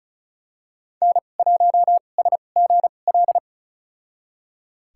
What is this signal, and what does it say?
Morse code, heard as a keyed tone, about 36 wpm, spaced as for 25 wpm, 700 Hz. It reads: N1SGL
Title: USA Call Signs - 2x Character Spacing - 35wpm